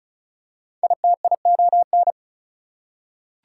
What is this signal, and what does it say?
Morse code, heard as a keyed tone, about 34 wpm, 700 Hz. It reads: ITION
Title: Prefixes and Suffixes 35wpm